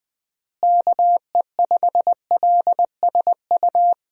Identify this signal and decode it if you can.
Morse code, heard as a keyed tone, about 20 wpm, 700 Hz. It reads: KE5LSU
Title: USA Call Signs - 6 Characters - Repeated 2x - 20wpm